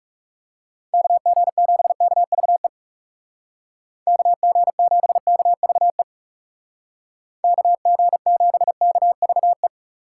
Morse code, heard as a keyed tone, about 36 wpm, 700 Hz. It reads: KG7KVE KG7KVE KG7KVE
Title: Speed Racing - USA 2x3 Call Signs 30wpm